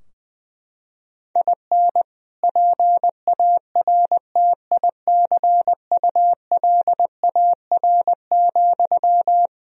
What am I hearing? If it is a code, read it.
Morse code, heard as a keyed tone, about 20 wpm, 700 Hz. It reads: IN PARTICULAR,
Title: Sentences from Top 1000 Words - Review Part 4 - 20wpm